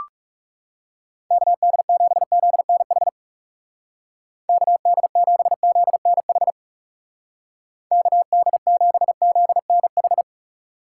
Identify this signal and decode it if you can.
Morse code, heard as a keyed tone, about 38 wpm, 700 Hz. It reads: KD7ZNH KD7ZNH KD7ZNH